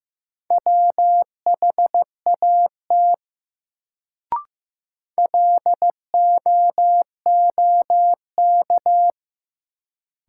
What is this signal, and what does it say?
Morse code, heard as a keyed tone, about 15 wpm, 700 Hz. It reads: WHAT LOOK
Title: Sentences from Top 100 Words - New Word Drill - 15wpm